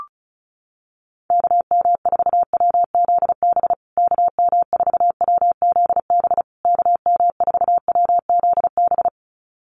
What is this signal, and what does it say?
Morse code, heard as a keyed tone, about 34 wpm, 700 Hz. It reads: KM4WZB KM4WZB KM4WZB
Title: USA Call Signs Repeated 3x 35wpm